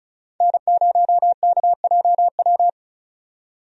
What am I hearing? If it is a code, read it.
Morse code, heard as a keyed tone, about 34 wpm, 700 Hz. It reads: N0KJW
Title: USA Call Signs - 5 Characters - 35wpm